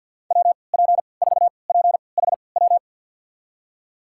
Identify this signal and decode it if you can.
Morse code, heard as a keyed tone, about 47 wpm, spaced as for 28 wpm, 700 Hz. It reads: WP4PHW
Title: USA Call Signs - 6 Characters - 3x Character Spacing - 50wpm